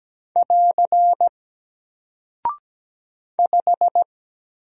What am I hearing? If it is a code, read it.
Morse code, heard as a keyed tone, about 17 wpm, 700 Hz. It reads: <AR> 5
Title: Single Character (Letter:Number:Punctuation:Prosign) Alt 17wpm